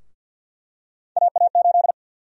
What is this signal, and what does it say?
Morse code, heard as a keyed tone, about 50 wpm, 700 Hz. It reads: AA8
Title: USA Call Sign Prefixes - 50wpm